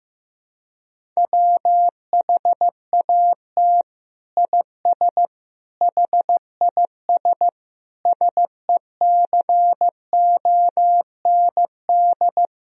Morse code, heard as a keyed tone, about 15 wpm, 700 Hz. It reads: WHAT IS HIS SECOND
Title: Sentences from Top 1000 Words - Review Part 1 - 15wpm